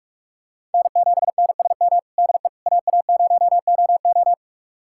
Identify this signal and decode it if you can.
Morse code, heard as a keyed tone, about 46 wpm, 700 Hz. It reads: N7NSM DE AA0OO
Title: USA Callsign DE Callsign - 5 Characters - 45wpm